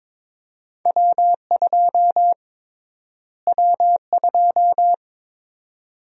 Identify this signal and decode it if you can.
Morse code, heard as a keyed tone, about 21 wpm, 700 Hz. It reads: W2 W2
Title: USA Call Sign Prefixes - Repeated 2x - 22wpm